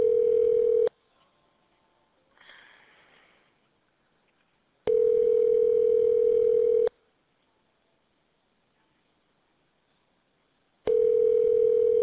Ringback tone (North America), 440 + 480 Hz.